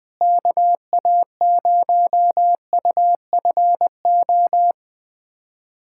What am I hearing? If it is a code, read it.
Morse code, heard as a keyed tone, about 21 wpm, 700 Hz. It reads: KA0UFO
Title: USA Callsign DE Callsign 20wpm